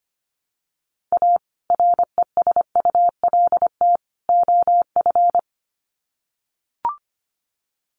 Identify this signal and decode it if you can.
Morse code, heard as a keyed tone, about 25 wpm, 700 Hz. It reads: A RESULT OF
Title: Sets of 3 Words 25wpm